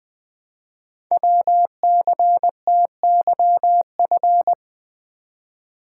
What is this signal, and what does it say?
Morse code, heard as a keyed tone, about 20 wpm, 700 Hz. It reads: WCTYF
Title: Code Groups - 5 Letter:Numbers - 20wpm